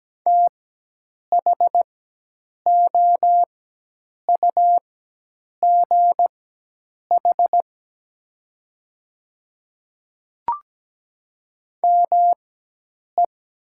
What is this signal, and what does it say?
Morse code, heard as a keyed tone, about 17 wpm, spaced as for 8 wpm, 700 Hz. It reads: THOUGH ME
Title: 4x Character Spacing - Top 500 Words - 17wpm